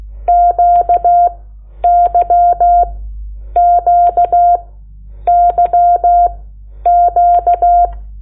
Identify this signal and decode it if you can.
Morse code, heard as a keyed tone, about 16 wpm, spaced as for 9 wpm, 700 Hz. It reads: QYQYQ
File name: Lecciones / CLASE Nº8